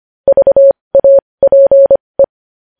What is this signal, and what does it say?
Morse code, heard as a keyed tone, about 25 wpm, spaced as for 20 wpm, 550 Hz. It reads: VAPE
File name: podcast / News headlines-1x-25x20